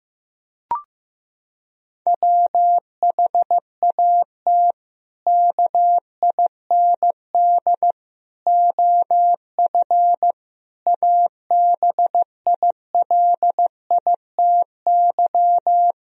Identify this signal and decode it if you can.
Morse code, heard as a keyed tone, about 15 wpm, 700 Hz. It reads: WHAT KIND OF ABILITY
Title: Sentences from Top 1000 Words - Review Part 3 - 15wpm